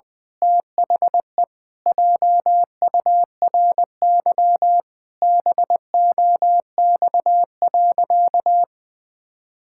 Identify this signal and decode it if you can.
Morse code, heard as a keyed tone, about 20 wpm, 700 Hz. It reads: THE JURY BOX.
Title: Pangram Challenge v2 - 20wpm